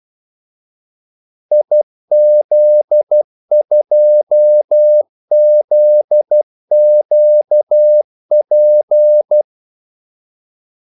Morse code, heard as a keyed tone, about 12 wpm, 600 Hz. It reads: IZ2ZQP
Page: 2 ekstra ordmellemrum mellem kaldesignalerne